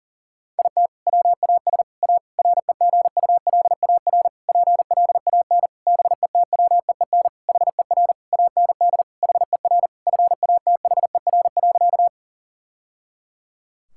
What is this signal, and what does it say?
Morse code, heard as a keyed tone, about 37 wpm, 700 Hz. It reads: IT WAS A REGULAR PLAN BETWEEN HER AND HER FATHER.
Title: Sentences from Top 1000 Words - Review Part 6 - 40wpm